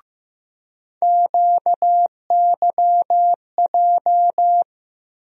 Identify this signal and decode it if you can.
Morse code, heard as a keyed tone, about 15 wpm, 700 Hz. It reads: QYJ
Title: USA Call Sign Postfixes - 3 Letters 15wpm